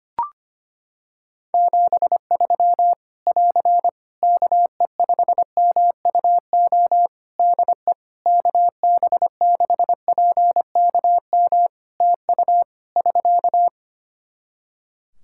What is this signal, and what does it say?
Morse code, heard as a keyed tone, about 25 wpm, 700 Hz. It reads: 73 <AR> KE5MUO DE KB6PKM TU <SK>
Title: Practice QSO - Exchange 4 - Ending - 25wpm